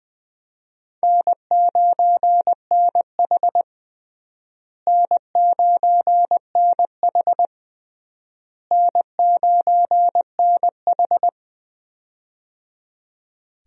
Morse code, heard as a keyed tone, about 21 wpm, 700 Hz. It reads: N9NH N9NH N9NH
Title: USA Call Signs - 4 Characters - Repeated 3x - 20wpm